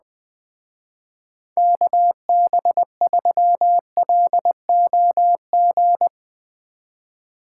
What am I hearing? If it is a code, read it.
Morse code, heard as a keyed tone, about 21 wpm, 700 Hz. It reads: KB3LOG